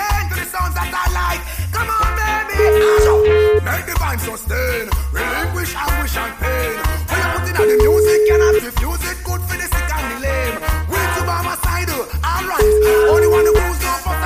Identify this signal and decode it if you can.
Ringback tone (Europe, 425 Hz), with noise.